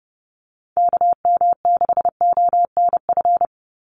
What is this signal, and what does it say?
Morse code, heard as a keyed tone, about 30 wpm, 700 Hz. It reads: KM6ONF
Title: USA Call Signs 30wpm